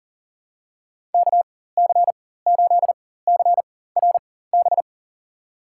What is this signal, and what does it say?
Morse code, heard as a keyed tone, about 39 wpm, spaced as for 18 wpm, 700 Hz. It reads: KC8CRB
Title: USA Call Signs - 6 Characters - 4x Character Spacing - 40wpm